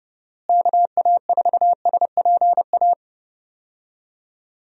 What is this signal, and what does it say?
Morse code, heard as a keyed tone, about 30 wpm, 700 Hz. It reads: KA4SPA
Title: USA Call Signs - 6 Characters - Repeated 2x - 30wpm